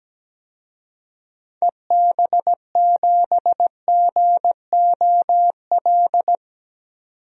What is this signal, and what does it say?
Morse code, heard as a keyed tone, about 17 wpm, 700 Hz. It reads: EB7GOL
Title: DXCC-100 Call Signs - 6 Characters - 17wpm